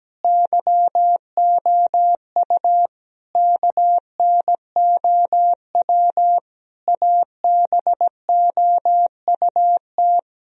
Morse code, heard as a keyed tone, about 17 wpm, 700 Hz. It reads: YOU KNOW ABOUT
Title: Sentences from Top 1000 Words - Review Part 3 - 17wpm